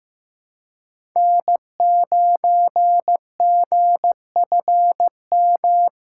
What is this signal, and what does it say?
Morse code, heard as a keyed tone, about 15 wpm, 700 Hz. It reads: N9GFM
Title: USA Callsign DE Callsign 15wpm